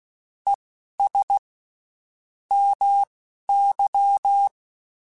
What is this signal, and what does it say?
Morse code, heard as a keyed tone, about 16 wpm, spaced as for 11 wpm, 800 Hz. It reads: ES MY
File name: (spaced) / QSOs Spaced-mp3s / WD0BGP 1 @15 sp